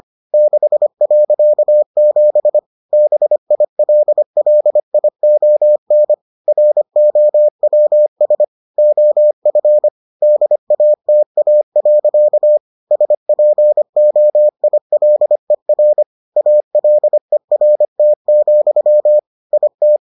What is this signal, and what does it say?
Morse code, heard as a keyed tone, about 25 wpm, 600 Hz. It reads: 6.7 BILLION ROWS OF DATA. SPOILER ALERT, IT